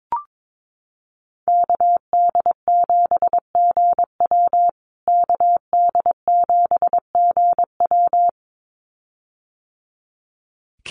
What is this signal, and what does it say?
Morse code, heard as a keyed tone, about 22 wpm, 700 Hz. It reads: KD7GW KD7GW
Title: USA Call Signs Repeated 2x 22wpm